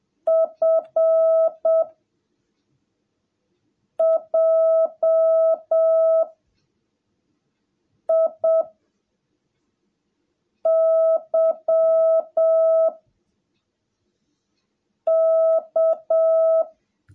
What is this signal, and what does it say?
Morse code, heard as a keyed tone, about 7 wpm, 660 Hz. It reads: F J I Y K
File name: jh8cbh / 2021cw / cw17a